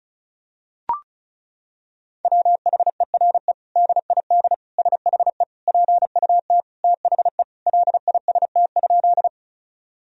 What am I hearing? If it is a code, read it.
Morse code, heard as a keyed tone, about 36 wpm, 700 Hz. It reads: WHERE DID SHE PUT THE LIST?
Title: Sentences from Top 1000 Words - Review Part 2 - 35wpm